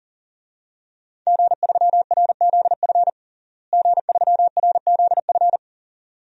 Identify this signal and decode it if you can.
Morse code, heard as a keyed tone, about 40 wpm, 700 Hz. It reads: G3RZF G3RZF
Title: DXCC-100 Call Signs - 5 Characters - Repeated 2x - 40wpm